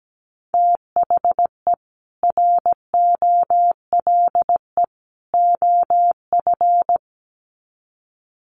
Morse code, heard as a keyed tone, about 17 wpm, 700 Hz. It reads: THE ROLE OF
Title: Sets of 3 Words 17wpm